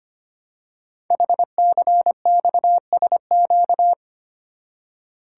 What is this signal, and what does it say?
Morse code, heard as a keyed tone, about 25 wpm, 700 Hz. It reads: HCXSQ
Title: Code Groups - 5 Letters - 25wpm